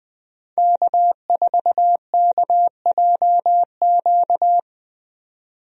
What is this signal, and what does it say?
Morse code, heard as a keyed tone, about 20 wpm, 700 Hz. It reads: K4KJQ
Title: USA Call Signs - 5 Characters - 20wpm